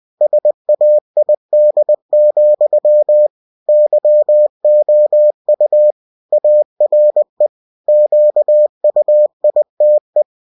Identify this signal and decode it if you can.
Morse code, heard as a keyed tone, about 20 wpm, 600 Hz. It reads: SAID, YOU ARE QUITE
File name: cw-books / winnie-the-pooh / winnie-20wpm-page-0065